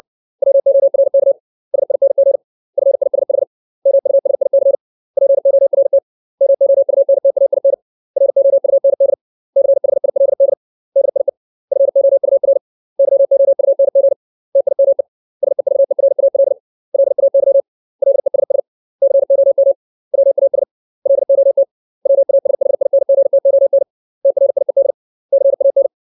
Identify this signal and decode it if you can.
Morse code, heard as a keyed tone, about 60 wpm, 550 Hz. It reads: WORK SING FISH MUSIC PORT MOUNTAIN ROUND CHILD DIE FOUR YOUNG TIME SEVERAL LAY LESS COM WAS LOT WASHINGTON TRIED CAN
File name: 632 words / 632 Words 60WPM